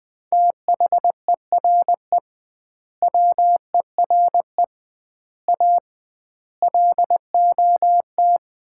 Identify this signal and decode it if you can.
Morse code, heard as a keyed tone, about 21 wpm, 700 Hz. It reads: THERE WERE A LOT
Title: 2x Word-Spacing - Sets of 5 Words - 20wpm